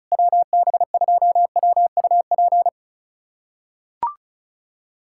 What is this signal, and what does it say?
Morse code, heard as a keyed tone, about 34 wpm, 700 Hz. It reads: WB2WUP
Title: USA Callsign DE Callsign - 6 Characters - 35wpm